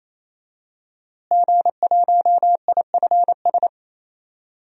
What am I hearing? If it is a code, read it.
Morse code, heard as a keyed tone, about 28 wpm, 700 Hz. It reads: G1IFS
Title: DXCC-100 Call Signs - 5 Characters - Repeated 3x - 28wpm